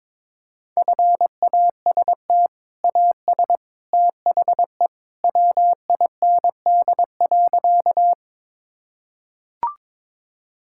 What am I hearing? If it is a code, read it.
Morse code, heard as a keyed tone, about 22 wpm, 700 Hz. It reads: FAST AS THE WIND.
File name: practice / 22wpm / Sentences with 2-Word Phrases - 22wpm